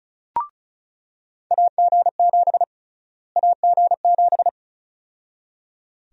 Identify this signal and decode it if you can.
Morse code, heard as a keyed tone, about 34 wpm, 700 Hz. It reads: AG7 AG7
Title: USA Call Sign Prefixes - Repeated 2x - 35wpm